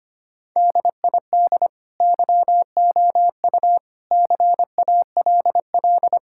Morse code, heard as a keyed tone, about 25 wpm, 700 Hz. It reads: DID YOU CALL
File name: practice / 25wpm / Sentences with 2-Word Phrases - 25wpm